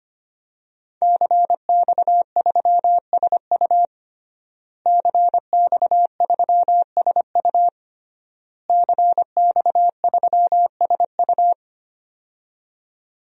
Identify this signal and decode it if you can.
Morse code, heard as a keyed tone, about 25 wpm, 700 Hz. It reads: CX3SU CX3SU CX3SU
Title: DXCC-100 Call Signs - 5 Characters - Repeated 3x - 25wpm